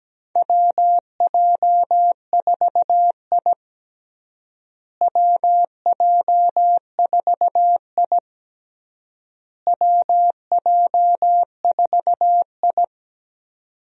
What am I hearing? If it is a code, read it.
Morse code, heard as a keyed tone, about 17 wpm, 700 Hz. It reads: WJ4I WJ4I WJ4I
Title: USA Call Signs - 4 Characters - Repeated 3x - 17wpm